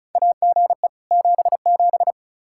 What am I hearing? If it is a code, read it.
Morse code, heard as a keyed tone, about 36 wpm, 700 Hz. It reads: AGE 77
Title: Practice QSO - Exchange 3 - More Info - 35wpm